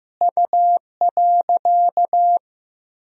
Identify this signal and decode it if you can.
Morse code, heard as a keyed tone, about 15 wpm, 700 Hz. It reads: U.
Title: Sentences from Top 400 Words and Abbreviations 15wpm